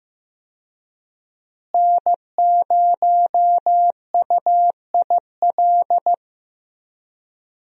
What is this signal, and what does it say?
Morse code, heard as a keyed tone, about 15 wpm, 700 Hz. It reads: N0UIL
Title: USA Call Signs - 5 Characters - Repeated 2x - 15wpm